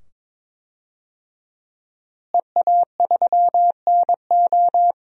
Morse code, heard as a keyed tone, about 22 wpm, 700 Hz. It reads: EA3NO
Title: DXCC-100 Call Signs - 5 Characters - Repeated 2x - 22wpm